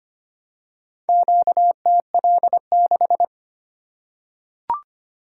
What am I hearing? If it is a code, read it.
Morse code, heard as a keyed tone, about 25 wpm, 700 Hz. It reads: QTL6
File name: practice / 25wpm / Four Letter-Number Combinations - 25wpm